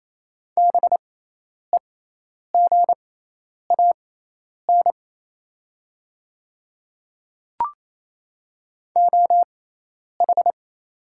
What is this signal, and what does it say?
Morse code, heard as a keyed tone, about 28 wpm, spaced as for 10 wpm, 700 Hz. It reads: BEGAN OH
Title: 6x Character Spacing - Top 500 Words - 28wpm